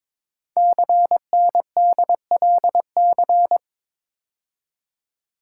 Morse code, heard as a keyed tone, about 23 wpm, 700 Hz. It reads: CNDLC
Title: Code Groups - 5 Letters - 22wpm